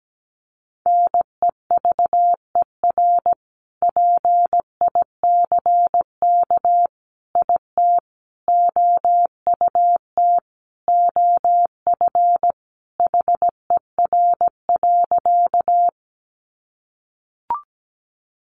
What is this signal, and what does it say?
Morse code, heard as a keyed tone, about 17 wpm, 700 Hz. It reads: NEVER PICK IT OUT OF HER.
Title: Sentences from Top 1000 Words 17wpm